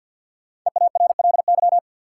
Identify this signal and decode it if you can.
Morse code, heard as a keyed tone, about 53 wpm, 700 Hz. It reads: EARLY
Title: Sentences from Top 400 Words - New Word Drill - 50wpm